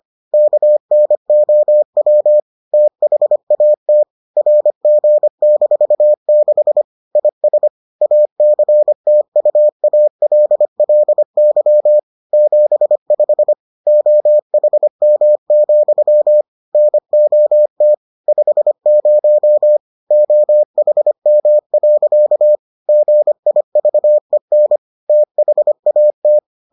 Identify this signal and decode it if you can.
Morse code, heard as a keyed tone, about 25 wpm, 600 Hz. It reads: KNOW THAT RG-6 IS ACTUALLY 75 OHM, NOT 50 OHM. GIVEN THAT